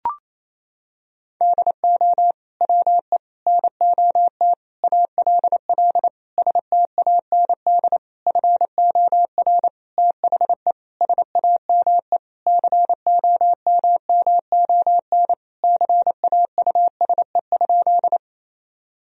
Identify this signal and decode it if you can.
Morse code, heard as a keyed tone, about 28 wpm, 700 Hz. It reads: DO WE NOT ALL STAND FOR THE SAME COMMON CAUSE?
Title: Sentences from Top 1000 Words - Review Part 3 - 28wpm